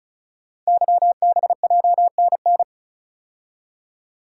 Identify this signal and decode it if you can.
Morse code, heard as a keyed tone, about 36 wpm, 690 Hz. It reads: YBJNN
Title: Code Groups - 5 Letters - 35wpm